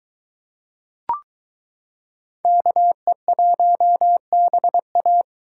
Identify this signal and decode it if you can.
Morse code, heard as a keyed tone, about 22 wpm, 700 Hz. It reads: KE1BA